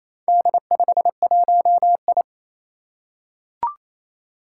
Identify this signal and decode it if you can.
Morse code, heard as a keyed tone, about 28 wpm, 700 Hz. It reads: D51I
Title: Four Challenging Letter-Number Combinations 28wpm